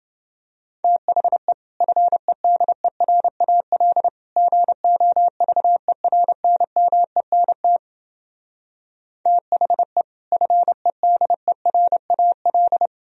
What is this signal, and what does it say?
Morse code, heard as a keyed tone, about 29 wpm, 700 Hz. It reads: THE FEDERAL GOVERNMENT THE FEDERAL